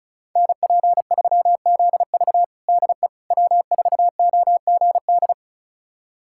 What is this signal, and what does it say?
Morse code, heard as a keyed tone, about 34 wpm, 700 Hz. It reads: NP3ZV DE W4OGD
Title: USA Callsign DE Callsign - 5 Characters - 35wpm